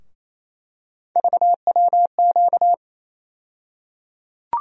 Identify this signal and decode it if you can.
Morse code, heard as a keyed tone, about 28 wpm, 700 Hz. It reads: VWQ
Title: USA Call Sign Postfixes - 3 Letters 28wpm